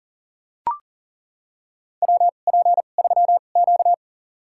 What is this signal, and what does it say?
Morse code, heard as a keyed tone, about 40 wpm, spaced as for 29 wpm, 700 Hz. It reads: WP3Q
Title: USA Call Signs - 4 Characters - 2x Character Spacing - 40wpm